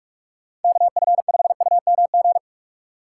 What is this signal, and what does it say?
Morse code, heard as a keyed tone, about 44 wpm, 700 Hz. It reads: KF5UMG
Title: USA Call Signs - 6 Characters - 6x Character Spacing - 45wpm